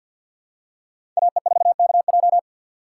Morse code, heard as a keyed tone, about 51 wpm, 700 Hz. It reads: AE4KJ
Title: USA Call Signs - 5 Characters - 4x Character Spacing - 50wpm